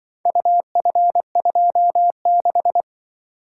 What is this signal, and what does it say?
Morse code, heard as a keyed tone, about 23 wpm, 700 Hz. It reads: UF26
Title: Speed-Racing - Four Letter-Number Combinations - 20wpm